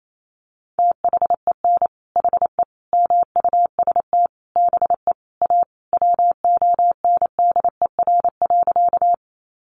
Morse code, heard as a keyed tone, about 28 wpm, 700 Hz. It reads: THEN HE MUST BE A WONDER.
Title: Sentences from Top 500 Words 28wpm